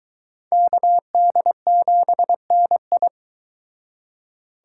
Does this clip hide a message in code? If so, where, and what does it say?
Yes: Morse code from 0.5 to 3.1 s, heard as a keyed tone, about 23 wpm, 700 Hz. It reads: KD7NI